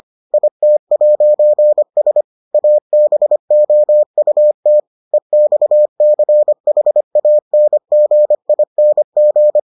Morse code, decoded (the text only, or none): IT'S ABOUT EXCHANGING